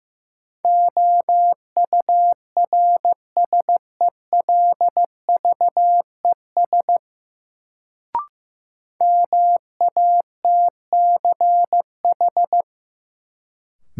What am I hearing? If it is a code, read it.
Morse code, heard as a keyed tone, about 15 wpm, 700 Hz. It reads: OURSELVES MATCH
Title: Sentences from Top 1600 Words - New Word Drill Alt - 15wpm